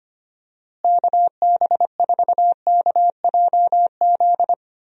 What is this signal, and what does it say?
Morse code, heard as a keyed tone, about 25 wpm, 700 Hz. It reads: KB4KJZ
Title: USA Call Signs - 6 Characters - Repeated 3x - 25wpm